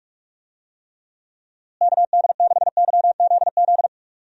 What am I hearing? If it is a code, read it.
Morse code, heard as a keyed tone, about 46 wpm, 700 Hz. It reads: KD6YZZ
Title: USA Call Signs - 6 Characters - Repeated 3x - 45wpm